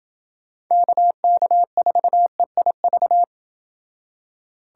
Morse code, heard as a keyed tone, about 29 wpm, 700 Hz. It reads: KK4EIV